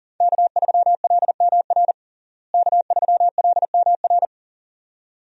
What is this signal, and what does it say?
Morse code, heard as a keyed tone, about 40 wpm, 700 Hz. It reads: K3LMR K3LMR
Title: USA Call Signs - 5 Characters - Repeated 2x - 40wpm